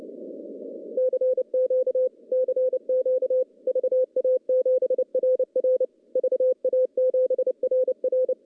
Morse code, heard as a keyed tone, about 30 wpm, 510 Hz. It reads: CQ CQ VA7RR VA7RR
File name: contests / CW-SS-k5tr- -0900